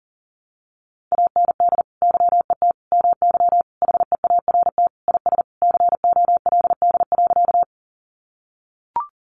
Morse code, heard as a keyed tone, about 40 wpm, 700 Hz. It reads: AND YET MY HEART IS COLD.
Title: Sentences from Top 1200 Words 40wpm